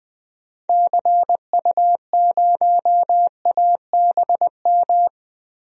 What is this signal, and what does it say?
Morse code, heard as a keyed tone, about 20 wpm, 700 Hz. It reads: CU0ABM
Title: DXCC-100 Call Signs - 6 Characters - 20wpm